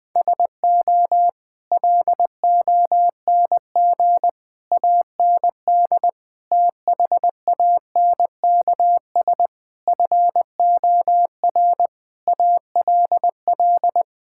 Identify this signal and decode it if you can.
Morse code, heard as a keyed tone, about 20 wpm, 700 Hz. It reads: SO LONG AND THANKS FOR ALL
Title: Sets of 8 Words - 20wpm